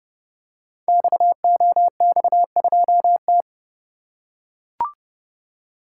Morse code, heard as a keyed tone, about 30 wpm, 700 Hz. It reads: XOX2T
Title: Code Groups - 5 Letter:Numbers - 30wpm